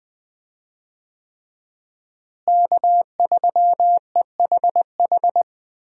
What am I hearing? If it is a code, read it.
Morse code, heard as a keyed tone, about 20 wpm, 700 Hz. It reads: K3EHH
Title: USA Call Signs - 5 Characters - Repeated 3x - 20wpm